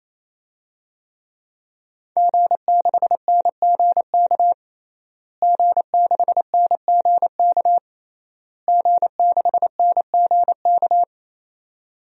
Morse code, heard as a keyed tone, about 28 wpm, 700 Hz. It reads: G6NGK G6NGK G6NGK
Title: DXCC-100 Call Signs - 5 Characters - Repeated 3x - 28wpm